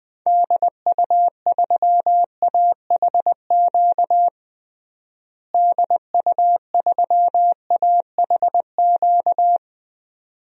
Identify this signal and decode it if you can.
Morse code, heard as a keyed tone, about 20 wpm, 700 Hz. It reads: DU3AHQ DU3AHQ
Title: DXCC-100 Call Signs - 6 Characters - Repeated 2x - 20wpm